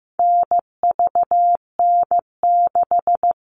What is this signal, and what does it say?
Morse code, heard as a keyed tone, about 15 wpm, 700 Hz. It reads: NVN6
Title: All Cut Numbers - 4-Digit 15wpm